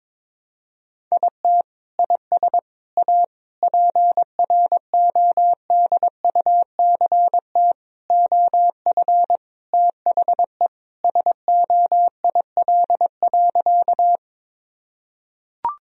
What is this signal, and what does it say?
Morse code, heard as a keyed tone, about 22 wpm, 700 Hz. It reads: IT IS A PRODUCT OF THE SOIL.
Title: Sentences from Top 1000 Words - Review Part 4 - 22wpm